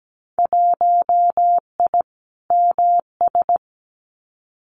Morse code, heard as a keyed tone, about 17 wpm, 700 Hz. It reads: 1I MS
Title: Winter Field Day Exchanges Encore 17wpm